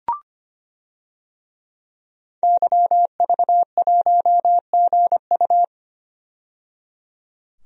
Morse code, heard as a keyed tone, about 25 wpm, 700 Hz. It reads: YV1GU
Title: DXCC-100 Call Signs - 5 Characters - 25wpm